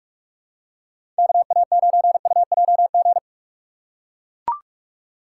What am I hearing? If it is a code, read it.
Morse code, heard as a keyed tone, about 41 wpm, 700 Hz. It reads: KA9UJG
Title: USA Call Signs - 6 Characters - 6x Character Spacing - 45wpm